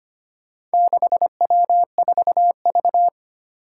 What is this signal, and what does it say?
Morse code, heard as a keyed tone, about 25 wpm, 700 Hz. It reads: 6W4V
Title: Four Challenging Letter-Number Combinations 25wpm